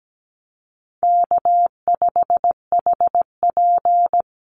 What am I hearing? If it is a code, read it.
Morse code, heard as a keyed tone, about 17 wpm, 700 Hz. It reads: K5HP
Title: Call Signs Repeated 3x - 17wpm